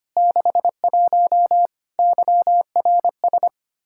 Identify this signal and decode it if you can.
Morse code, heard as a keyed tone, about 25 wpm, 700 Hz. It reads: 61 YRS
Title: Practice QSO - Exchange 3 - More Info - 25wpm